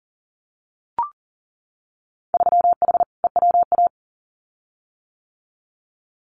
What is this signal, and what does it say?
Morse code, heard as a keyed tone, about 41 wpm, 700 Hz. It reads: 3H EWA
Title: Winter Field Day Exchanges Encore 40wpm